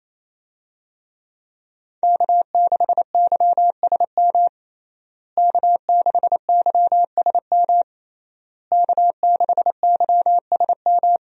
Morse code, heard as a keyed tone, about 27 wpm, 700 Hz. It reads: K6YSM K6YSM K6YSM
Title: USA Call Signs - 5 Characters - Repeated 3x - 28wpm